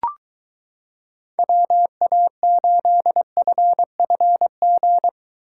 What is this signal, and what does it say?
Morse code, heard as a keyed tone, about 22 wpm, 700 Hz. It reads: WA8FFG